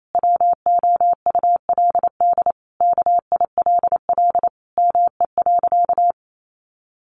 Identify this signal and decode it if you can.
Morse code, heard as a keyed tone, about 28 wpm, 700 Hz. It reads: WOULD KILL ME.
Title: Sentences from Top 1400 Words 28wpm